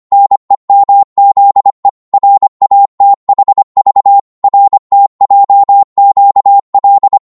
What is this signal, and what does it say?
Morse code, heard as a keyed tone, about 25 wpm, 800 Hz. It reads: NEMZE RATHV RTJQL